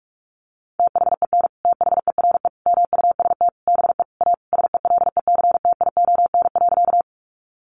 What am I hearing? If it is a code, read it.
Morse code, heard as a keyed tone, about 44 wpm, 700 Hz. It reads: THEN THERE MUST BE A SELECTION.
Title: Sentences from Top 1400 Words 45wpm